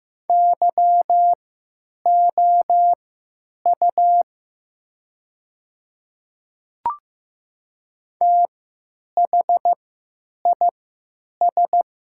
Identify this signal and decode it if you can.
Morse code, heard as a keyed tone, about 15 wpm, spaced as for 9 wpm, 700 Hz. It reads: YOU THIS
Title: 3x Character Spacing - Top 100 Words - 15wpm